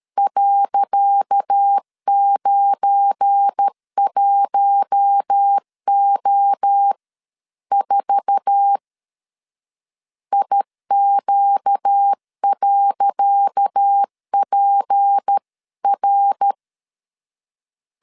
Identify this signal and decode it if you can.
Morse code, heard as a keyed tone, about 12 wpm, 800 Hz. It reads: .91O4 IQ.PR
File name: LEZIONI / 10 SIGNS / 10C DOT 12wpm 2451